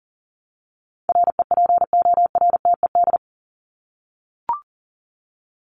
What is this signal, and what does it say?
Morse code, heard as a keyed tone, about 40 wpm, 700 Hz. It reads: REPORTED
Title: ED Suffix - Top 100 Words with Base Words 40wpm